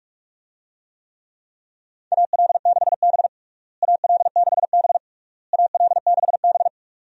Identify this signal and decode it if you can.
Morse code, heard as a keyed tone, about 41 wpm, 700 Hz. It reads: AL6B AL6B AL6B
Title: USA Call Signs - 4 Characters - Repeated 3x - 45wpm